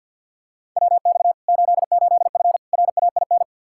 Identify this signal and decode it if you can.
Morse code, heard as a keyed tone, about 49 wpm, 700 Hz. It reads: WX 88F RAIN
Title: Practice QSO - Exchange 2 - Info - 50wpm